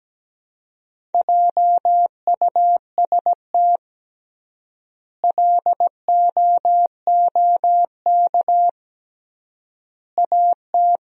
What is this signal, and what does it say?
Morse code, heard as a keyed tone, about 17 wpm, 700 Hz. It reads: JUST LOOK AT
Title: 3x Word-Spacing - Sentences from Top 200 Words - 17wpm